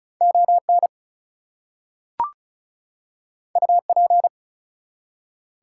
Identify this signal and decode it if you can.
Morse code, heard as a keyed tone, about 36 wpm, 700 Hz. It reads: ON UP
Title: Two Letter Words - 35wpm